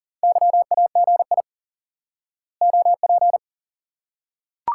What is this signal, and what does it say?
Morse code, heard as a keyed tone, about 40 wpm, 700 Hz. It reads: YAGI OP
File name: IQE Course 3 - Lesson 89 - YAGI - Two Elements - 40wpm